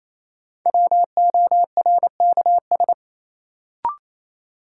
Morse code, heard as a keyed tone, about 28 wpm, 700 Hz. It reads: WORKS
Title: Sentences from Top 1000 Words - Review Part 5 - New Word Drill - 28wpm